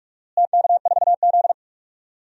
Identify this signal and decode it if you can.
Morse code, heard as a keyed tone, about 41 wpm, 700 Hz. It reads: TK4Z
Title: DXCC-100 Call Signs - 4 Characters - Repeated 2x - 45wpm